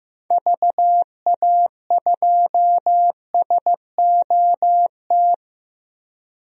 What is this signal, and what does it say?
Morse code, heard as a keyed tone, about 15 wpm, 700 Hz. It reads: VA2SOT
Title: DXCC-100 Call Signs - 6 Characters - 15wpm